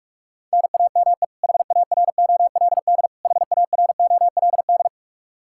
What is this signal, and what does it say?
Morse code, heard as a keyed tone, about 41 wpm, 700 Hz. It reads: NAME HAROLD HAROLD
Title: Practice QSO - Exchange 1 - Beginning - 45wpm